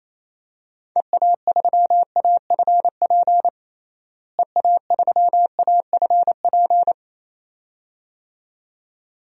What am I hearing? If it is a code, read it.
Morse code, heard as a keyed tone, about 28 wpm, 700 Hz. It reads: EA3AFP EA3AFP
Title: DXCC-100 Call Signs - 6 Characters - Repeated 2x - 28wpm